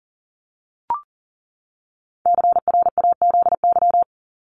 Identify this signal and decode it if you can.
Morse code, heard as a keyed tone, about 40 wpm, 700 Hz. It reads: CRAZY
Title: Top 1501-1750 Words 40wpm